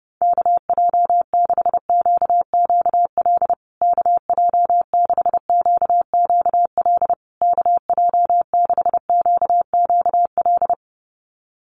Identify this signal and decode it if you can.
Morse code, heard as a keyed tone, about 30 wpm, 700 Hz. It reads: KJ6QQL KJ6QQL KJ6QQL
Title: USA Call Signs Repeated 3x 30wpm